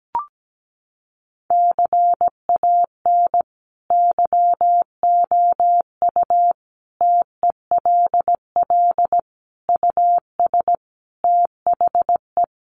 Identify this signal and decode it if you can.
Morse code, heard as a keyed tone, about 17 wpm, 700 Hz. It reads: CAN YOU TELL US THE
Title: Sentences from Top 1300 Words 17wpm